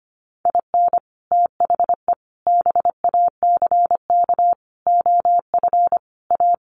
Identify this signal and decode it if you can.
Morse code, heard as a keyed tone, about 25 wpm, 700 Hz. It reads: IN THE BACK OF A
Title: Sets of 5 Words Encore2 25wpm